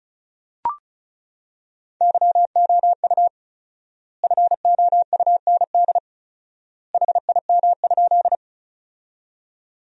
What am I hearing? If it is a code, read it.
Morse code, heard as a keyed tone, about 34 wpm, 700 Hz. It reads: YOU FOUND HIM?
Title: 4x Word-Spacing - Sentences from Top 300 Words - 35wpm